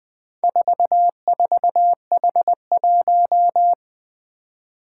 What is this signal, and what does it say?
Morse code, heard as a keyed tone, about 21 wpm, 700 Hz. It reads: 44H1
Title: Four Challenging Letter-Number Combinations 20wpm